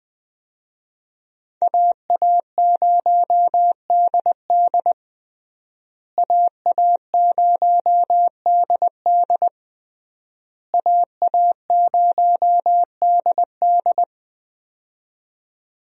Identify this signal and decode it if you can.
Morse code, heard as a keyed tone, about 20 wpm, 700 Hz. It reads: AA0DD AA0DD AA0DD
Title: USA Call Signs - 5 Characters - Repeated 3x - 20wpm